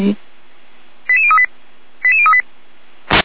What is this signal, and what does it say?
Five-tone selective call (ZVEI), heard as quick runs of five tones, tones of about 70 ms, with speech. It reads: 88038 88038
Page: SDR/ZVEI